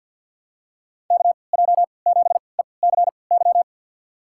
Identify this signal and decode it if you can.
Morse code, heard as a keyed tone, about 50 wpm, spaced as for 28 wpm, 700 Hz. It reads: KJ7ECY
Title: USA Call Signs - 6 Characters - 3x Character Spacing - 50wpm